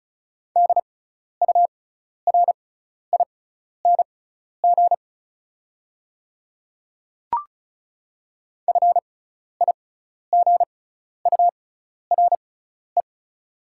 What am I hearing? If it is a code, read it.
Morse code, heard as a keyed tone, about 36 wpm, spaced as for 12 wpm, 700 Hz. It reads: DURING FIGURE
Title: 6x Character Spacing - Top 500 Words - 35wpm